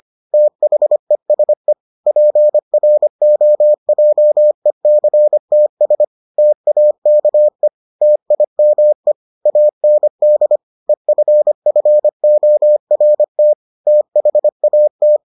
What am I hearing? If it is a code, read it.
Morse code, heard as a keyed tone, about 25 wpm, 600 Hz. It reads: THESE PROJECTS TAKE TIME AND EFFORT THAT